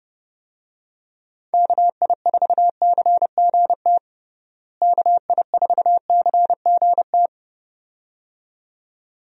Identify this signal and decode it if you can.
Morse code, heard as a keyed tone, about 30 wpm, 700 Hz. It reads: KI4CGT KI4CGT
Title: USA Call Signs - 6 Characters - Repeated 2x - 30wpm